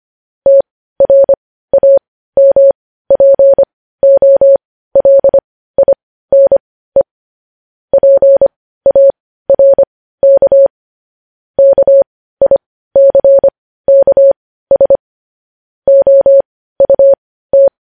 Morse code, heard as a keyed tone, about 25 wpm, spaced as for 15 wpm, 550 Hz. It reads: TRAMPOLINE PARK KICKS OUT
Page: news headlines in morse code at 15 wpm